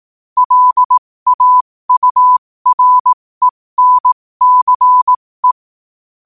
Morse code, heard as a keyed tone, about 18 wpm, spaced as for 15 wpm, 1000 Hz. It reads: LAURENCE
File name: qotdfiles / 15wpm